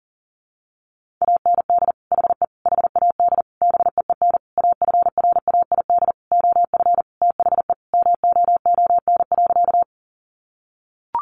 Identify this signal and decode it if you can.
Morse code, heard as a keyed tone, about 40 wpm, 700 Hz. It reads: AND HE HAD BEEN AFRAID OF THE MOON.
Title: Sentences from Top 1500 Words 40wpm